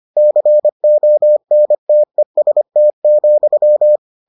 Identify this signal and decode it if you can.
Morse code, heard as a keyed tone, about 25 wpm, 600 Hz. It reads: CONTEST,